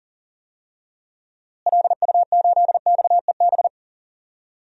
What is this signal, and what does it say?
Morse code, heard as a keyed tone, about 37 wpm, 700 Hz. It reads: LU8XEB
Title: DXCC-100 Call Signs - 6 Characters - Repeated 2x - 40wpm